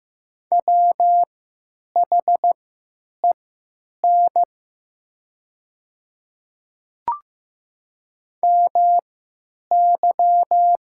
Morse code, heard as a keyed tone, about 15 wpm, spaced as for 9 wpm, 700 Hz. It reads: WHEN MY
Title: 3x Character Spacing - Top 100 Words - 15wpm